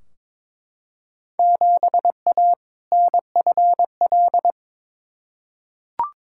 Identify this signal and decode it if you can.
Morse code, heard as a keyed tone, about 22 wpm, 700 Hz. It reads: 7A NFL
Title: Field Day Exchanges Encore - 22wpm